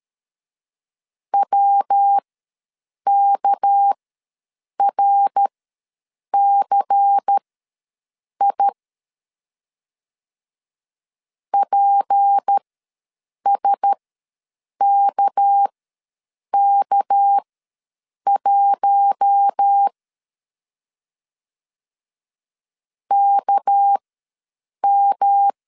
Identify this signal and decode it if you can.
Morse code, heard as a keyed tone, about 13 wpm, spaced as for 7 wpm, 800 Hz. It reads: WKRCI PSKK1 KM
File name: LEZIONI / 6 PKX0 / 6 PK ws 12wpm 3116